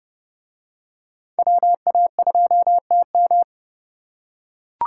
Morse code, heard as a keyed tone, about 29 wpm, 700 Hz. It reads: WA2TM